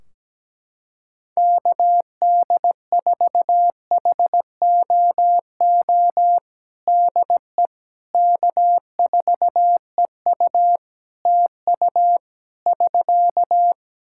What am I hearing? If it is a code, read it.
Morse code, heard as a keyed tone, about 17 wpm, 700 Hz. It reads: KD4HOO DE K4EU TU <SK>
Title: Practice QSO - Exchange 4 - Ending - 17wpm